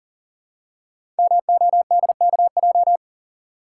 Morse code, heard as a keyed tone, about 42 wpm, 700 Hz. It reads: MODKJ
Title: Code Groups - 5 Letter:Numbers - 40wpm